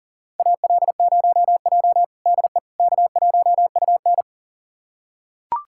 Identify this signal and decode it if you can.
Morse code, heard as a keyed tone, about 40 wpm, 700 Hz. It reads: AL0J DE K1UN
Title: USA Callsign DE Callsign - 4 Characters - 40wpm